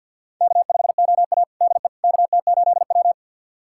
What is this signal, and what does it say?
Morse code, heard as a keyed tone, about 53 wpm, 700 Hz. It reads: K5OA DE KT8W
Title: USA Callsign DE Callsign - 4 Characters - 50wpm